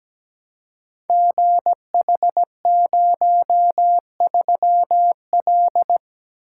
Morse code, heard as a keyed tone, about 17 wpm, 690 Hz. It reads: GH03L
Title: Code Groups - 5 Letter:Numbers - 17wpm